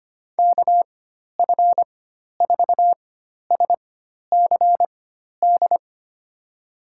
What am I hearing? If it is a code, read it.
Morse code, heard as a keyed tone, about 25 wpm, spaced as for 12 wpm, 700 Hz. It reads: KF4SCD
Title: USA Call Signs - 6 Characters - 4x Character Spacing - 25wpm